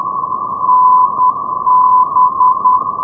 Morse code, heard as a keyed tone, about 10 wpm, 1010 Hz. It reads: NB
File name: europe / isl / isl-214 ndb akureyri-botn nb-387-np-isl-5x-2012-04-rb